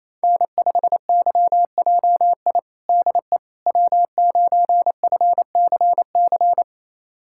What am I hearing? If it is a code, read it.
Morse code, heard as a keyed tone, about 27 wpm, 700 Hz. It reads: N5YJI DE W9FCC
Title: USA Callsign DE Callsign - 5 Characters - 28wpm